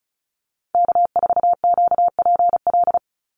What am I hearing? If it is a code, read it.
Morse code, heard as a keyed tone, about 36 wpm, 700 Hz. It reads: K4QPL
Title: Call Signs - 35wpm